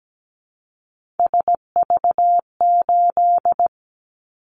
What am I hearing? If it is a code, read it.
Morse code, heard as a keyed tone, about 17 wpm, 700 Hz. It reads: SV8
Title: Callsign Prefixes for Top 50 DXCC - 17wpm